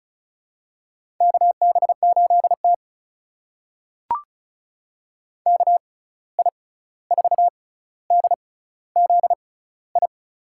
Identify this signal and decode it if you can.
Morse code, heard as a keyed tone, about 36 wpm, spaced as for 16 wpm, 700 Hz. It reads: KB8T KI4DZI
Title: USA Call Signs - 6x Character Spacing - 35wpm